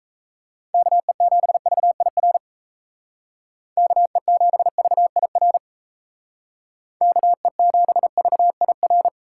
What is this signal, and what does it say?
Morse code, heard as a keyed tone, about 35 wpm, 700 Hz. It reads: KE7VIR KE7VIR KE7VIR
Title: Speed Racing - USA 2x3 Call Signs 28wpm